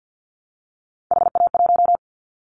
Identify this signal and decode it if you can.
Morse code, heard as a keyed tone, about 50 wpm, 700 Hz. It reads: HA1
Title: Callsign Prefixes for Top 50 DXCC - 50wpm